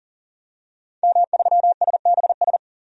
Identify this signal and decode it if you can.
Morse code, heard as a keyed tone, about 40 wpm, 700 Hz. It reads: M3SBS
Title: DXCC-100 Call Signs - 5 Characters - 40wpm